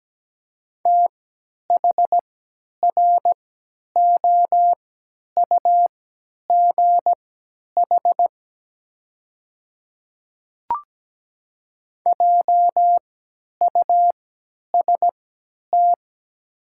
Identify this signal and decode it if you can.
Morse code, heard as a keyed tone, about 17 wpm, spaced as for 10 wpm, 700 Hz. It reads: THROUGH JUST
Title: 3x Character Spacing - Top 300 Words - 17wpm